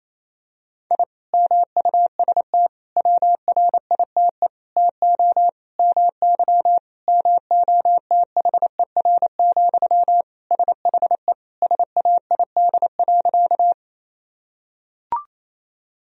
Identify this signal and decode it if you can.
Morse code, heard as a keyed tone, about 28 wpm, 700 Hz. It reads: I MUST WRITE TO MY MOTHER, SHE SAID.
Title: Sentences from Top 1000 Words - Review Part 1 - 28wpm